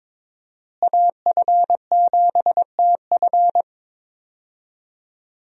Morse code, heard as a keyed tone, about 21 wpm, 700 Hz. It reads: AF7TF
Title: USA Call Signs - 5 Characters - Repeated 2x - 22wpm